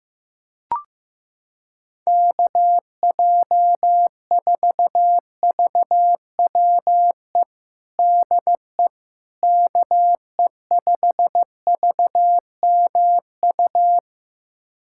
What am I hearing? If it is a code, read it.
Morse code, heard as a keyed tone, about 15 wpm, 700 Hz. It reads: KJ4VWE DE KE5VMU
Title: USA Callsign DE Callsign - 6 Characters - 15wpm